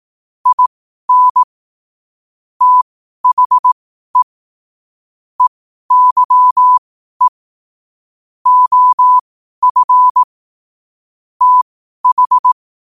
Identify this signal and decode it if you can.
Morse code, heard as a keyed tone, about 18 wpm, spaced as for 12 wpm, 1000 Hz. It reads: IN THE EYE OF TH
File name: qotdfiles / 12wpm